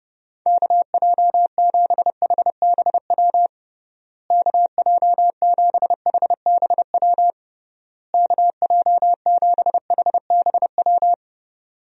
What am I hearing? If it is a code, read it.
Morse code, heard as a keyed tone, about 30 wpm, 700 Hz. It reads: KJ7HBW KJ7HBW KJ7HBW
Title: USA Call Signs - 6 Characters - Repeated 3x - 30wpm